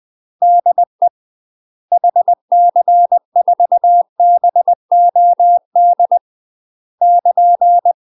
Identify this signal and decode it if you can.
Morse code, heard as a keyed tone, about 20 wpm, 700 Hz. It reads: DE HC4BOD <KN>
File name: UK MASTER QSOs TRAINING / 20WPM / UK MASTER QSOs 0013